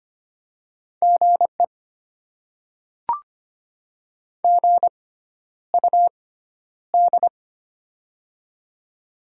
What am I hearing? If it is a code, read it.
Morse code, heard as a keyed tone, about 25 wpm, spaced as for 11 wpm, 690 Hz. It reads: GE GUD
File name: Top 20 QSO Elements - 6x Character Spacing - 25wpm